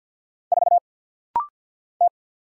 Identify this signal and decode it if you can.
Morse code, heard as a keyed tone, about 50 wpm, 690 Hz. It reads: 4 T
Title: Single Letter-Number - Rapid-Fire 50wpm